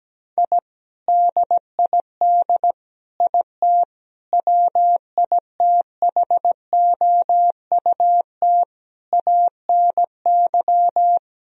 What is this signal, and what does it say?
Morse code, heard as a keyed tone, about 17 wpm, 700 Hz. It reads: I DID IT WITHOUT ANY
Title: Sentences from Top 1000 Words - Review Part 3 - 17wpm